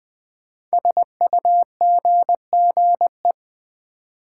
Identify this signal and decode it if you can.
Morse code, heard as a keyed tone, about 20 wpm, 700 Hz. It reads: SUGGE
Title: Build A Word - 5 Letters At A Time - 10-Letter Words - 20wpm